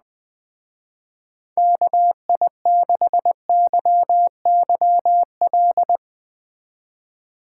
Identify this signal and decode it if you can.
Morse code, heard as a keyed tone, about 20 wpm, 700 Hz. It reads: KI6YYL
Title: Speed Racing - USA 2x3 Call Signs 17wpm